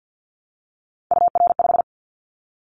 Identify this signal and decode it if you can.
Morse code, heard as a keyed tone, about 50 wpm, 700 Hz. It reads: UR5
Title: Callsign Prefixes for Top 25 DXCC - 50wpm